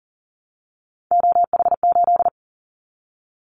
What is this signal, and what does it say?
Morse code, heard as a keyed tone, about 42 wpm, 700 Hz. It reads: OH8
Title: Callsign Prefixes for Top 50 DXCC - 40wpm